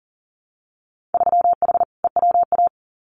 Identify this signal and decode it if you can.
Morse code, heard as a keyed tone, about 40 wpm, 700 Hz. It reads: 3H EWA
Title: Winter Field Day Exchanges Encore 40wpm